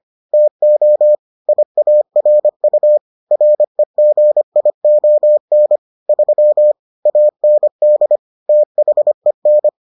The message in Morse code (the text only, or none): TO IARU REGION 3 AND THEN